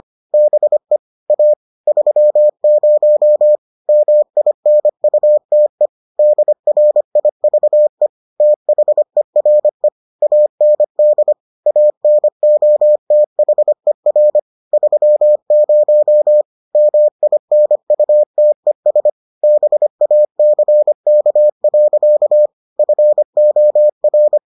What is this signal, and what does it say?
Morse code, heard as a keyed tone, about 25 wpm, 600 Hz. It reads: BE A 30 MINUTE DRIVE THERE AND ANOTHER 30 MINUTES BACK. FOR